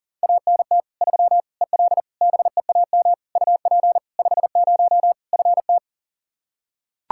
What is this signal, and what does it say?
Morse code, heard as a keyed tone, about 40 wpm, 700 Hz. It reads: ANT 3 EL BEAM UP 50 FT
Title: Practice QSO - Exchange 2 - Info - 40wpm